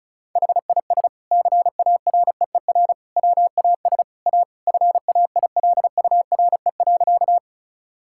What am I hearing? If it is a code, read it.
Morse code, heard as a keyed tone, about 36 wpm, 700 Hz. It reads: HIS CAREER WAS A FAILURE.
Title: Sentences from Top 1000 Words - Review Part 6 - 35wpm